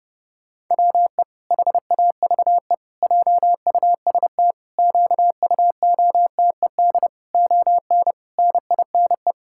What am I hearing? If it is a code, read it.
Morse code, heard as a keyed tone, about 28 wpm, 700 Hz. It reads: WE HAVE JUST QUOTED ON NINE
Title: Pangram Challenge v2 - 30wpm